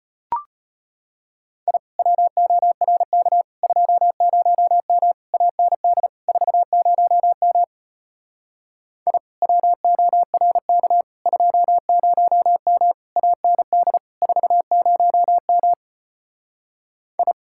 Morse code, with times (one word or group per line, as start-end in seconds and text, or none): 1.7-1.8 I
2.0-3.4 WORK
3.6-5.1 20M
5.3-6.1 AND
6.3-7.6 40M
9.1-9.2 I
9.4-11.0 WORK
11.3-12.9 20M
13.2-14.0 AND
14.2-15.7 40M
17.2-17.3 I